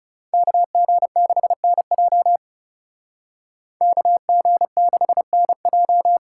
Morse code, heard as a keyed tone, about 38 wpm, 700 Hz. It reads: KG6NJ KG6NJ